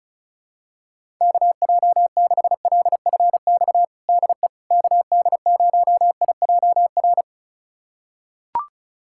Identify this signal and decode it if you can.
Morse code, heard as a keyed tone, about 36 wpm, 700 Hz. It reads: KJ6LFX DE KD0IJR
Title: USA Callsign DE Callsign - 6 Characters - 35wpm